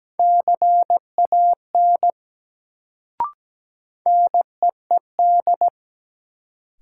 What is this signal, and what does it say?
Morse code, heard as a keyed tone, about 17 wpm, 700 Hz. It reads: CAN NEED
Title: Sets of 2 Words - New Word Drill - 17wpm